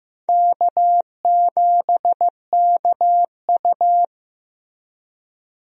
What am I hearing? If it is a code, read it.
Morse code, heard as a keyed tone, about 15 wpm, 700 Hz. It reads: K7KU
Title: USA Call Signs - 4 Characters - Repeated 2x - 15wpm